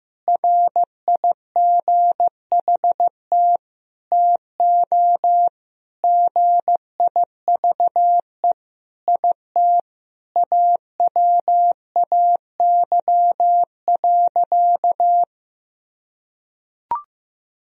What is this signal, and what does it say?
Morse code, heard as a keyed tone, about 15 wpm, 700 Hz. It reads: RIGHT TO GIVE IT AWAY.
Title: Sentences from Top 1000 Words - Review Part 1 - 15wpm